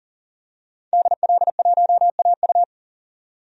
Morse code, heard as a keyed tone, about 38 wpm, 700 Hz. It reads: DL1AU
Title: DXCC-100 Call Signs - 5 Characters - 40wpm